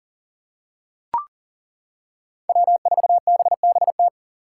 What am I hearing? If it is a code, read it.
Morse code, heard as a keyed tone, about 40 wpm, 700 Hz. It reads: W4BBT
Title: K1USN SST QSO Practice - 40wpm